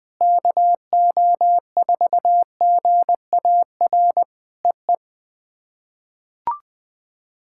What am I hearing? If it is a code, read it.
Morse code, heard as a keyed tone, about 20 wpm, 700 Hz. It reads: KO4GAR EE
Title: POTA QSO Practice - 20wpm